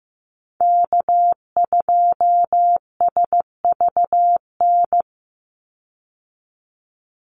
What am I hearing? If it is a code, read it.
Morse code, heard as a keyed tone, about 15 wpm, 700 Hz. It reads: K2SVN
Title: USA Call Signs 15wpm